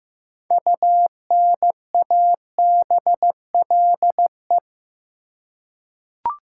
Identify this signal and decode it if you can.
Morse code, heard as a keyed tone, about 15 wpm, 700 Hz. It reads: UNABLE
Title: Six Letter Words - 15wpm